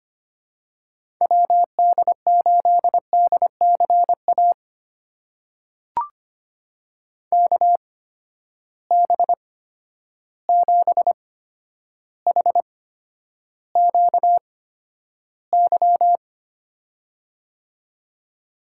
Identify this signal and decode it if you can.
Morse code, heard as a keyed tone, about 25 wpm, spaced as for 11 wpm, 700 Hz. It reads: WD8DCA KB7HQY
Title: USA Call Signs - 6 Characters - 8x Character Spacing - 25wpm